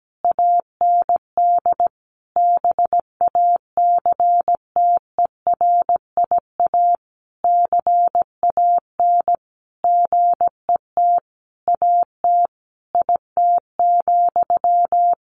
Morse code, heard as a keyed tone, about 17 wpm, 700 Hz. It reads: AND BACTERIA CAN GET AT IT,
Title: A Sound Of Thunder 17wpm